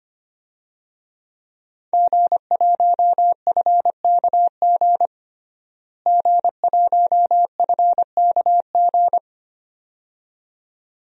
Morse code, heard as a keyed tone, about 25 wpm, 700 Hz. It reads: G1FKG G1FKG
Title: DXCC-100 Call Signs - 5 Characters - Repeated 2x - 25wpm